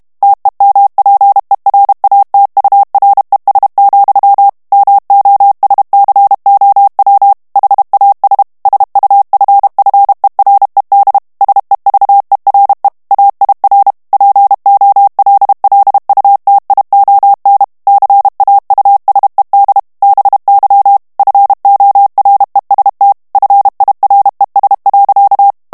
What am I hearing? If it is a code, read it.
Morse code, heard as a keyed tone, about 32 wpm, 800 Hz. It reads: TEMPERATURES, MOSCOW HAS SUFFERED SEVERE AIR POLLUTION CAUSED BY FOREST FIRES.
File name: ANGLAIS texte clair 2 vitesse 30 mots minute